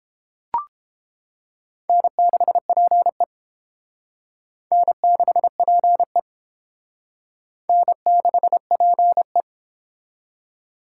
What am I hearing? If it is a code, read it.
Morse code, heard as a keyed tone, about 31 wpm, 700 Hz. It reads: N6PE N6PE N6PE